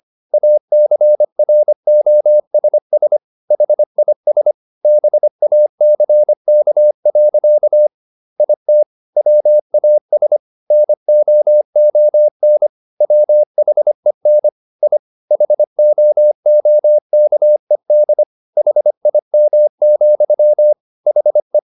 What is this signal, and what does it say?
Morse code, heard as a keyed tone, about 25 wpm, 600 Hz. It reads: ACROSS HIS BACK. IT WAS NOON WHEN I HOOKED HIM, HE
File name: the old man and the sea part 1 25wpm